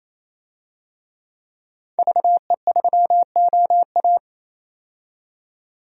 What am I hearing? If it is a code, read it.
Morse code, heard as a keyed tone, about 27 wpm, 700 Hz. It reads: VE3OA
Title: DXCC-100 Call Signs - 5 Characters - 28wpm